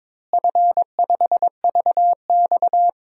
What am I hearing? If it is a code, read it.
Morse code, heard as a keyed tone, about 21 wpm, 700 Hz. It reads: F5VX
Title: DXCC-100 Call Signs - 4 Characters - Repeated 3x - 22wpm